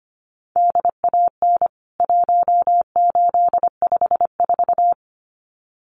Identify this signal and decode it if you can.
Morse code, heard as a keyed tone, about 25 wpm, 700 Hz. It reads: DAN 1854
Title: CWT Exchange 25wpm